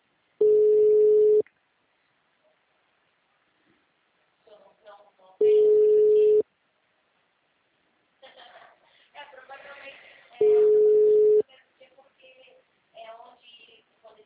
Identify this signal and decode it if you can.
Ringback tone (Europe, 425 Hz), with speech.